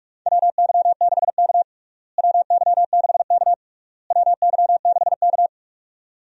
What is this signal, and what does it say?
Morse code, heard as a keyed tone, about 46 wpm, 700 Hz. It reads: WY6K WY6K WY6K
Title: USA Call Signs - 4 Characters - Repeated 3x - 45wpm